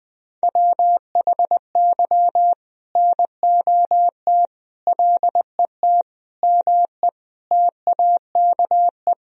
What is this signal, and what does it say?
Morse code, heard as a keyed tone, about 20 wpm, 700 Hz. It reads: WHY NOT LET ME TAKE
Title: Sentences from Top 1000 Words - Review Part 2 - 20wpm